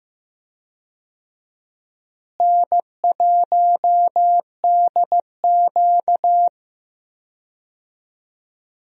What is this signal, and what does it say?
Morse code, heard as a keyed tone, about 15 wpm, 700 Hz. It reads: N1DQ
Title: USA Call Signs - 4 Characters - Repeated 2x - 15wpm